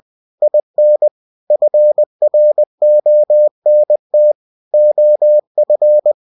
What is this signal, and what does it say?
Morse code, heard as a keyed tone, about 20 wpm, 600 Hz. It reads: IN FRONT OF